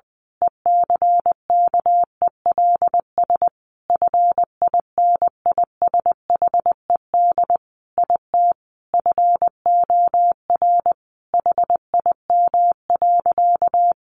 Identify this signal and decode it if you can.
Morse code, heard as a keyed tone, about 20 wpm, 700 Hz. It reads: ECKELS FINISHED IT FOR HIM.
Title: A Sound Of Thunder 20wpm